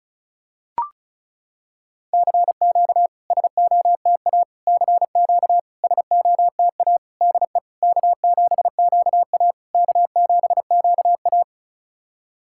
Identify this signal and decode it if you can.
Morse code, heard as a keyed tone, about 36 wpm, 700 Hz. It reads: CQ SOTA CQ SOTA DE K7QA K7QA
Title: SOTA QSO Practice 35wpm